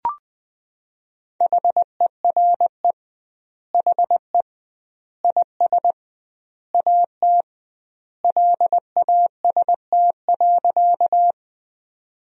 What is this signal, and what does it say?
Morse code, heard as a keyed tone, about 20 wpm, 700 Hz. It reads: HERE HE IS AT LAST.
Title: 2x Word-Spacing - Sentences from Top 300 Words - 20wpm